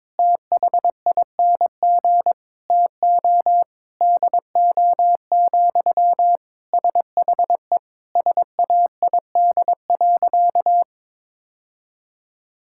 Morse code, heard as a keyed tone, about 22 wpm, 700 Hz. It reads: THING TO DO, SHE SAID.
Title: Sentences from Top 1000 Words - Review Part 5 - 22wpm